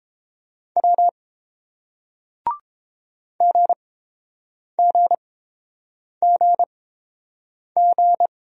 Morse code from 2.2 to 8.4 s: GGGG

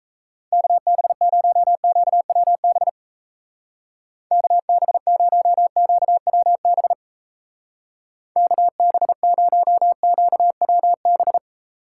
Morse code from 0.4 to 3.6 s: KB0QWB

